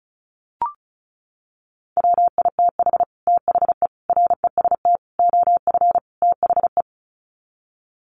Morse code, heard as a keyed tone, about 34 wpm, 700 Hz. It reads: WITH THE REST OF THE
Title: Sets of 5 Words 35wpm